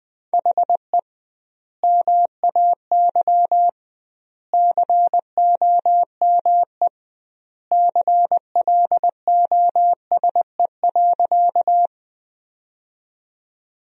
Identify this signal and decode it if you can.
Morse code, heard as a keyed tone, about 20 wpm, 700 Hz. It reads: HE MAY COME CLOSE.
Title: 2x Word-Spacing - Sentences from Top 300 Words - 20wpm